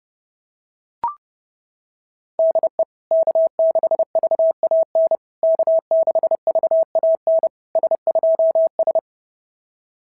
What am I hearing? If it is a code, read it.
Morse code, heard as a keyed tone, about 30 wpm, 650 Hz. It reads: DE K6VAN K6VAN S2S
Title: SOTA QSO Practice 30wpm